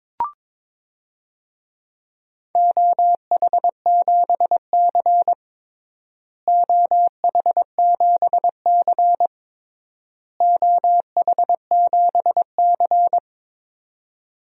Morse code, heard as a keyed tone, about 21 wpm, 700 Hz. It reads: OH7C OH7C OH7C
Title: DXCC-100 Call Signs - 4 Characters - Repeated 3x - 22wpm